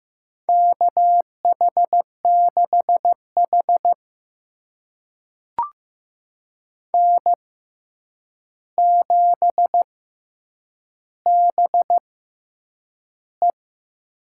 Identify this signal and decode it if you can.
Morse code, heard as a keyed tone, about 15 wpm, spaced as for 8 wpm, 700 Hz. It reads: KH6H N7BE
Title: USA Call Signs - 4 Characters - 6x Character Spacing - 15wpm